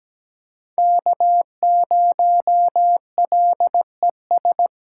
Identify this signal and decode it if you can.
Morse code, heard as a keyed tone, about 20 wpm, 700 Hz. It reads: K0LES
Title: Speed Racing - USA Call Signs 17wpm